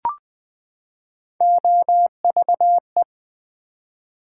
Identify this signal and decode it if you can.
Morse code, heard as a keyed tone, about 20 wpm, 700 Hz. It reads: OVE
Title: Build A Word - 3 Letters At A Time - 9-Letter Words - 20wpm